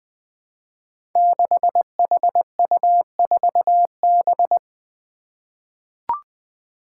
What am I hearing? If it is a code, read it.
Morse code, heard as a keyed tone, about 20 wpm, 700 Hz. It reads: 6HU4B
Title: Five Challenging Letter-Number Combinations 20wpm